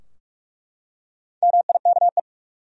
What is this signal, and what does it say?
Morse code, heard as a keyed tone, about 47 wpm, 700 Hz. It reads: MIKE
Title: USA Common Names - Max 4 Letters - 45wpm